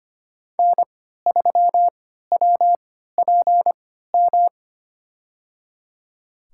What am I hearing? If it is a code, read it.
Morse code, heard as a keyed tone, about 25 wpm, spaced as for 14 wpm, 700 Hz. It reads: N3WPM
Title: USA Call Signs - 3x Character Spacing - 25wpm